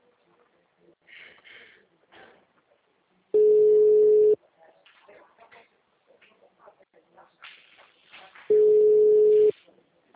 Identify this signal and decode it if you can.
Ringback tone (Europe, 425 Hz), with speech.